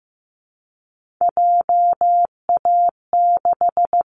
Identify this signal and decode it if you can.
Morse code, heard as a keyed tone, about 15 wpm, 700 Hz. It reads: JA6
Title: Callsign Prefixes for Top 25 DXCC - Repeated 3x - 15wpm